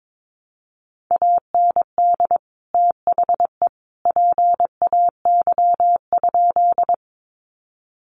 Morse code, heard as a keyed tone, about 22 wpm, 700 Hz. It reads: AND THE PAY?
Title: Sentences from Top 700 Words 22wpm